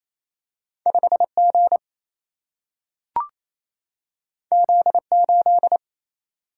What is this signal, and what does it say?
Morse code, heard as a keyed tone, about 28 wpm, 700 Hz. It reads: 5 G Z 8
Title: Single-Letter Single-Number 28wpm